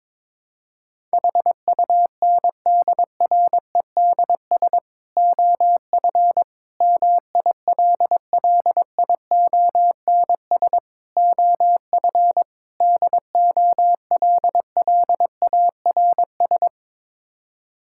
Morse code, heard as a keyed tone, about 22 wpm, 700 Hz. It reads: HUNDREDS OF MILLIONS OF DOLLARS
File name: practice / 22wpm / Build A Phrase - 5-Word Phrases - 22wpm